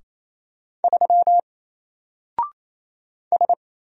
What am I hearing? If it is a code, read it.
Morse code, heard as a keyed tone, about 27 wpm, 700 Hz. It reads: 3 S
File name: Single-Letter-Number-28wpm